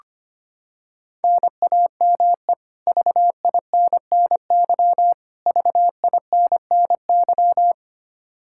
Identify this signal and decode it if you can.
Morse code, heard as a keyed tone, about 25 wpm, 700 Hz. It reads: NAME VINNY VINNY
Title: Practice QSO - Exchange 1 - Beginning - 25wpm